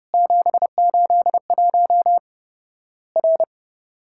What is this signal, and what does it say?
Morse code, heard as a keyed tone, about 30 wpm, 700 Hz. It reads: 781 R
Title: Worked All Europe DX Contest - QTC Exchange - 30wpm